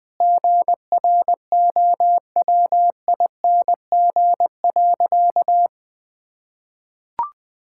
Morse code, heard as a keyed tone, about 20 wpm, 700 Hz. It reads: GROWING.
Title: Sentences with 8-Word Phrases - Spoken Beginning - 20wpm